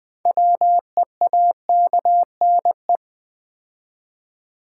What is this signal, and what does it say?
Morse code, heard as a keyed tone, about 20 wpm, 700 Hz. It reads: WEAKNE
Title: Build A Word - 3 Letters At A Time - 8-Letter Words - 20wpm